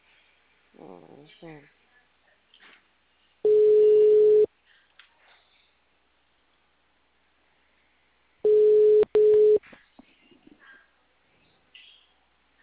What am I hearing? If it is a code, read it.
Ringback tone (Europe, 425 Hz), with speech.
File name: out- -2016- - - .89653